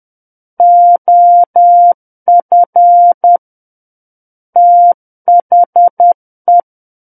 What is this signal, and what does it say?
Morse code, heard as a keyed tone, about 10 wpm, 700 Hz. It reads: OF THE
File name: Book One / 01-Chapter 1 - 10 WPM